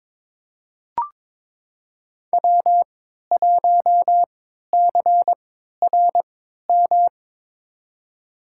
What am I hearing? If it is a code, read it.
Morse code, heard as a keyed tone, about 22 wpm, spaced as for 13 wpm, 700 Hz. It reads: W1CRM
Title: USA Call Signs - 3x Character Spacing - 22wpm